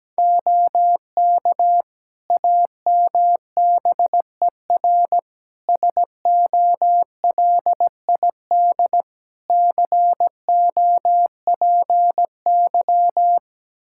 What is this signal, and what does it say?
Morse code, heard as a keyed tone, about 17 wpm, 700 Hz. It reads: OK AMBER SOLID COPY
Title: Practice QSO - Exchange 3 - More Info - 17wpm